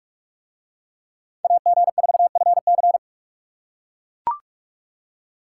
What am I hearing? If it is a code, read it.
Morse code, heard as a keyed tone, about 44 wpm, 700 Hz. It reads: AG4FC